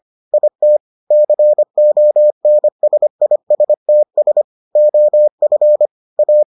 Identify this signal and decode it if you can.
Morse code, heard as a keyed tone, about 25 wpm, 600 Hz. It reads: IT CONSISTS OF A